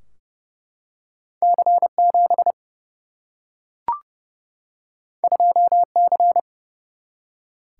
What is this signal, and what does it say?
Morse code, heard as a keyed tone, about 30 wpm, 700 Hz. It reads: C7 2C
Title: Single-Letter Single-Number 30wpm